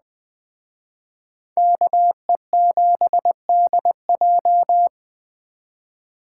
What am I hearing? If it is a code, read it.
Morse code, heard as a keyed tone, about 20 wpm, 700 Hz. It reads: KE7DJ
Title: Speed Racing - USA 2x2 Call Signs 17wpm